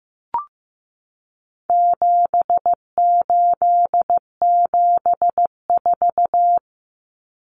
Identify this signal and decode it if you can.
Morse code, heard as a keyed tone, about 15 wpm, 700 Hz. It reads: 7874
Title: Common Cut Numbers - 4-Digit 15wpm